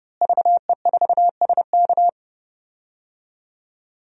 Morse code, heard as a keyed tone, about 30 wpm, 700 Hz. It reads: VE4SK
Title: DXCC-100 Call Signs - 5 Characters - Repeated 2x - 30wpm